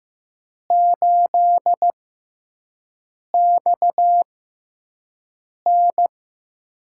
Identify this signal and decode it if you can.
Morse code, heard as a keyed tone, about 15 wpm, spaced as for 5 wpm, 700 Hz. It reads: 8XN
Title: USA Call Signs - 4 Characters - 6x Character Spacing - 15wpm